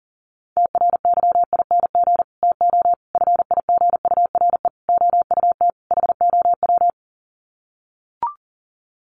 Morse code, heard as a keyed tone, about 40 wpm, 700 Hz. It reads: TRYING TO FIGURE OUT HOW
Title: Sets of 5 Words Encore 40wpm